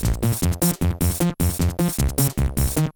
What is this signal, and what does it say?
There is no code.